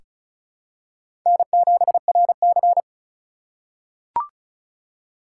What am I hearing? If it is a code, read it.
Morse code, heard as a keyed tone, about 36 wpm, 700 Hz. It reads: N7RC
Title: USA Call Signs - 3x Character Spacing - 35wpm